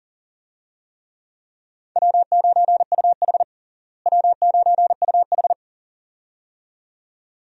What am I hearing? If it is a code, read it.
Morse code, heard as a keyed tone, about 37 wpm, 700 Hz. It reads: W9UH W9UH
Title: USA Call Signs - 4 Characters - Repeated 2x - 40wpm